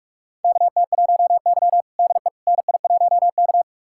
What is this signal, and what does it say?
Morse code, heard as a keyed tone, about 44 wpm, 690 Hz. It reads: KT1Y DE NI1K
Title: USA Callsign DE Callsign - 4 Characters - 45wpm